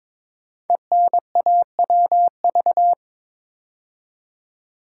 Morse code, heard as a keyed tone, about 22 wpm, 700 Hz. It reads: ENAWV
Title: Code Groups - 5 Letters - 22wpm